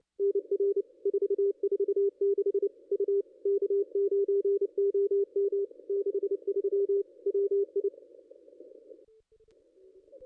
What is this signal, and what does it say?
Morse code, heard as a keyed tone, about 25 wpm, 440 Hz. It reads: NR 446 U K9OM 63 WI
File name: contests / ss-cw-k5tr-op-n5rz -0400